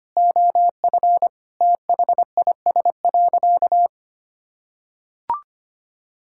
Morse code, heard as a keyed tone, about 25 wpm, 700 Hz. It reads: OF THIS.
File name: practice / 25wpm / Sentences with 3-Word Phrases - 25wpm